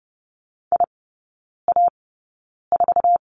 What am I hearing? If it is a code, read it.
Morse code, heard as a keyed tone, about 30 wpm, spaced as for 9 wpm, 700 Hz. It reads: IA4
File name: course / 30:10wpm / 087 Lesson 15e - Number 5 - Three Characters 30wpm